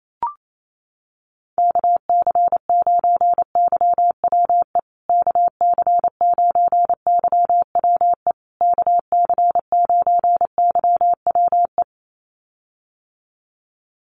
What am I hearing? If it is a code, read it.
Morse code, heard as a keyed tone, about 28 wpm, 690 Hz. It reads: KC9YWE KC9YWE KC9YWE
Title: USA Call Signs Repeated 3x 28wpm